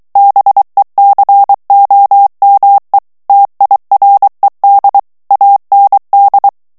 Morse code, heard as a keyed tone, about 23 wpm, 800 Hz. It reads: BECOME TIRED AND
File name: morse mp3 / ANGLAIS texte clair 1 vitesse 22 mots minute